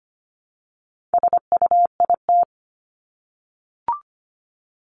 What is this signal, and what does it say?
Morse code, heard as a keyed tone, about 25 wpm, 700 Hz. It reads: SUIT
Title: Top 1501-1750 Words 25wpm